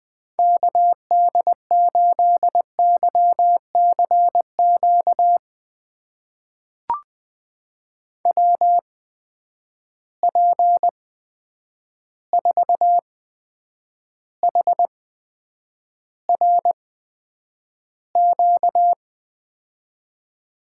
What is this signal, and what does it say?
Morse code, heard as a keyed tone, about 20 wpm, spaced as for 9 wpm, 700 Hz. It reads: KD8YCQ WP4HRQ
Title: USA Call Signs - 6 Characters - 8x Character Spacing - 20wpm